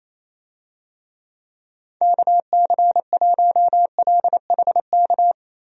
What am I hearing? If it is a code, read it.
Morse code, heard as a keyed tone, about 28 wpm, 700 Hz. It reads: KC1LHK
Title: USA Call Signs - 6 Characters - Repeated 3x - 28wpm